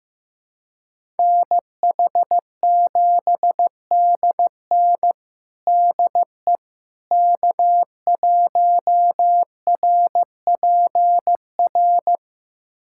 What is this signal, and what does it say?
Morse code, heard as a keyed tone, about 15 wpm, 700 Hz. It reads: NH7DN DE K1RPR
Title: USA Callsign DE Callsign 15wpm